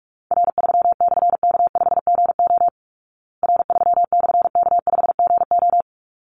Morse code, heard as a keyed tone, about 44 wpm, 700 Hz. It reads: R3/K5GO R3/K5GO
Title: Call Signs Repeated 3x - 45wpm